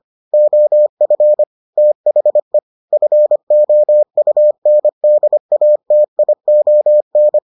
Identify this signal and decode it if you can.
Morse code, heard as a keyed tone, about 25 wpm, 600 Hz. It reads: OF THE FOUNDATION